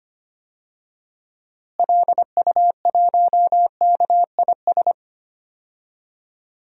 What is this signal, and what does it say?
Morse code, heard as a keyed tone, about 25 wpm, 700 Hz. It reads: LU1KIS
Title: DXCC-100 Call Signs - 6 Characters - Repeated 2x - 25wpm